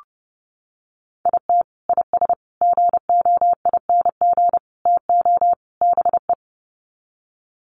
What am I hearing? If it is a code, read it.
Morse code, heard as a keyed tone, about 30 wpm, 700 Hz. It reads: IT IS GOING TO BE
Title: Sets of 5 Words 30wpm